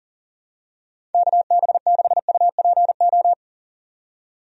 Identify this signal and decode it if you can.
Morse code, heard as a keyed tone, about 40 wpm, 700 Hz. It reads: KB6UPO
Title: USA Call Signs - 6 Characters - 40wpm